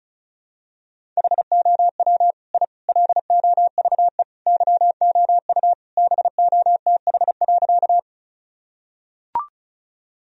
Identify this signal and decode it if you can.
Morse code, heard as a keyed tone, about 34 wpm, 700 Hz. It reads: HOW I LOVE YOU BOTH.
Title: Sentences from Top 1000 Words - Review Part 1 - 35wpm